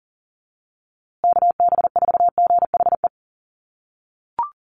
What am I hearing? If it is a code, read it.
Morse code, heard as a keyed tone, about 40 wpm, 700 Hz. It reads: KB4GHE
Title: USA Call Signs Repeated 3x 40wpm